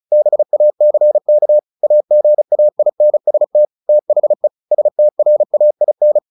Morse code, heard as a keyed tone, about 34 wpm, 600 Hz. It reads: BACK AGAINST THE STRAIN
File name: the old man and the sea / the old man and the sea 1 35wpm